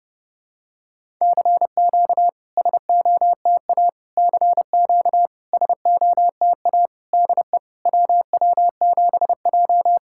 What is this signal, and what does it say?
Morse code, heard as a keyed tone, about 30 wpm, 690 Hz. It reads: CQ SOTA CQ SOTA DE WW7J
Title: SOTA QSO Practice 30wpm